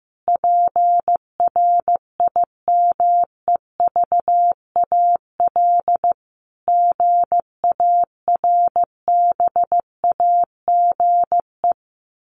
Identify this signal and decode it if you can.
Morse code, heard as a keyed tone, about 15 wpm, 700 Hz. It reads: PRIMEVAL GARBAGE
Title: A Sound Of Thunder 15wpm